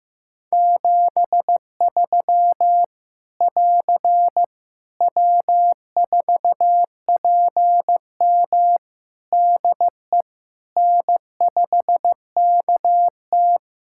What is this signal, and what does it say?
Morse code, heard as a keyed tone, about 15 wpm, 700 Hz. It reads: 73 <AR> W4PM DE N5KT
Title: K1USN SST QSO Practice 15wpm